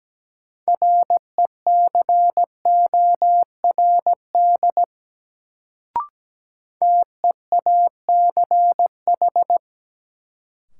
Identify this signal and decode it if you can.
Morse code, heard as a keyed tone, about 17 wpm, 700 Hz. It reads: RECORD TEACH
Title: Sentences from Top 500 Words - New Word Drill - 17wpm